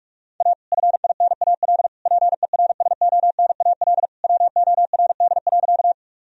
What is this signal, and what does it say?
Morse code, heard as a keyed tone, about 46 wpm, 700 Hz. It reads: A FINAL PERSONAL WORD.
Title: Sentences from Top 1000 Words - Review Part 4 - 45wpm